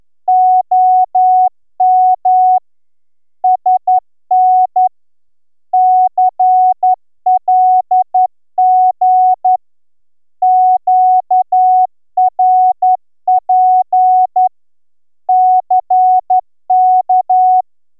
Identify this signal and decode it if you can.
Morse code, heard as a keyed tone, about 11 wpm, 740 Hz. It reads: OM SN CLG QRP CK